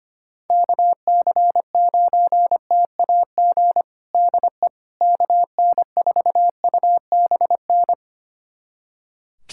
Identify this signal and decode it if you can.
Morse code, heard as a keyed tone, about 25 wpm, 700 Hz. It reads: KC9TAG DE KN4UBN
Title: USA Callsign DE Callsign - 6 Characters - 25wpm